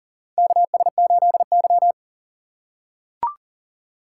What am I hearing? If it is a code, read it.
Morse code, heard as a keyed tone, about 39 wpm, 700 Hz. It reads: KS8Y
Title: USA Call Signs - 4 Characters - 3x Character Spacing - 40wpm